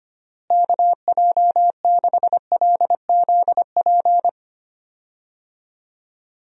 Morse code, heard as a keyed tone, about 25 wpm, 700 Hz. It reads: KJ6LZP
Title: USA Call Signs - 6 Characters - Repeated 2x - 25wpm